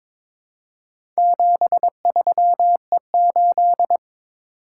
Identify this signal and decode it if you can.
Morse code, heard as a keyed tone, about 22 wpm, 700 Hz. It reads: 73E8
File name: practice / 22wpm / Four Letter-Number Combinations - 22wpm